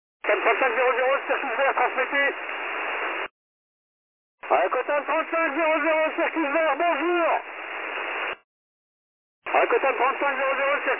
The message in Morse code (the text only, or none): none